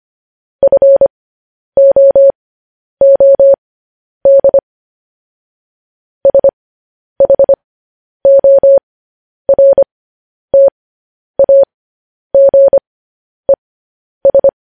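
Morse code, heard as a keyed tone, about 25 wpm, spaced as for 10 wpm, 550 Hz. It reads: FOOD SHORTAGES
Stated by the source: Morse code transmission of news headlines.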